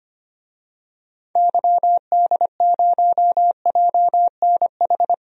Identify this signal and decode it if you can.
Morse code, heard as a keyed tone, about 25 wpm, 700 Hz. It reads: YD0JNH
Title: DXCC-100 Call Signs - 6 Characters - 25wpm